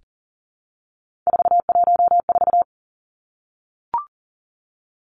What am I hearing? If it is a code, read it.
Morse code, heard as a keyed tone, about 40 wpm, 700 Hz. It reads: 414
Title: Numbers - 3-Digit 40wpm